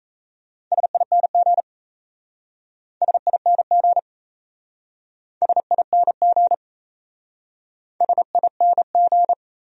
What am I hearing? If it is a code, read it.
Morse code, heard as a keyed tone, about 36 wpm, 700 Hz. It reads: SING SING SING SING